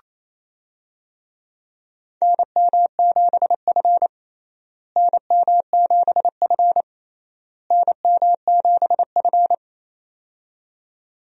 Morse code, heard as a keyed tone, about 27 wpm, 700 Hz. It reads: NM7F NM7F NM7F
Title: USA Call Signs - 4 Characters - Repeated 3x - 28wpm